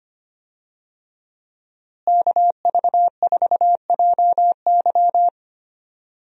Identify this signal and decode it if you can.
Morse code, heard as a keyed tone, about 25 wpm, 700 Hz. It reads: KV4JY
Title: USA Call Signs - 5 Characters - Repeated 3x - 25wpm